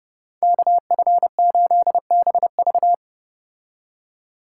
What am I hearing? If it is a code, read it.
Morse code, heard as a keyed tone, about 29 wpm, 700 Hz. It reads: KF8BV